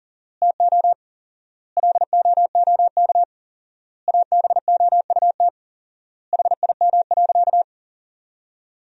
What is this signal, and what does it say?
Morse code, heard as a keyed tone, about 40 wpm, 700 Hz. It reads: TO LOOK ABOUT HIM.
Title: 4x Word-Spacing - Sentences from Top 200 Words - 40wpm